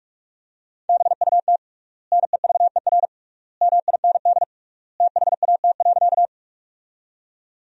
Morse code, heard as a keyed tone, about 46 wpm, 700 Hz. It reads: BUT NEVER MIND THAT.
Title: 3x Word-Spacing - Sentences from Top 500 Words - 45wpm